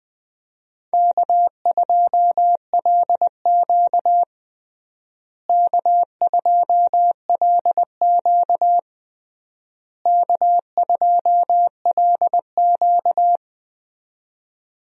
Morse code, heard as a keyed tone, about 20 wpm, 700 Hz. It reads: K2LQ K2LQ K2LQ
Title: USA Call Signs - 4 Characters - Repeated 3x - 20wpm